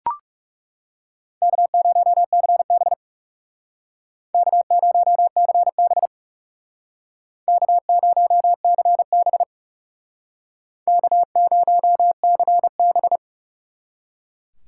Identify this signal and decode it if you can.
Morse code, heard as a keyed tone, about 38 wpm, 700 Hz. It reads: K0CB K0CB K0CB K0CB